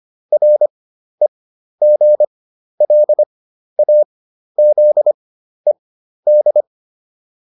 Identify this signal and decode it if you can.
Morse code, heard as a keyed tone, about 25 wpm, spaced as for 12 wpm, 600 Hz. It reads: REGLAZED